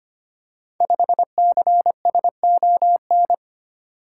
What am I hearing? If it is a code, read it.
Morse code, heard as a keyed tone, about 25 wpm, 700 Hz. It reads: 5CSON
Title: Code Groups - 5 Letter:Numbers - 25wpm